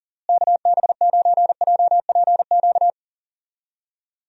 Morse code, heard as a keyed tone, about 39 wpm, 700 Hz. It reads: KB9JPQ
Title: USA Call Signs - 6 Characters - Repeated 3x - 40wpm